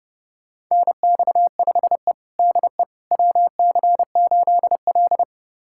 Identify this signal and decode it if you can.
Morse code, heard as a keyed tone, about 30 wpm, 700 Hz. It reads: NX5E DE WC8L
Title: USA Callsign DE Callsign - 4 Characters - 30wpm